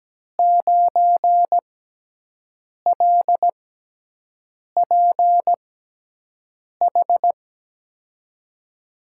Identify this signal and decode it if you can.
Morse code, heard as a keyed tone, about 17 wpm, spaced as for 6 wpm, 700 Hz. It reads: 9LPH
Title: USA Call Signs - 6 Characters - 6x Character Spacing - 17wpm